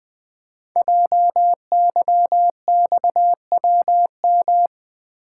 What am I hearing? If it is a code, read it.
Morse code, heard as a keyed tone, about 20 wpm, 700 Hz. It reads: JYXWM
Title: Code Groups - 5 Letters - 20wpm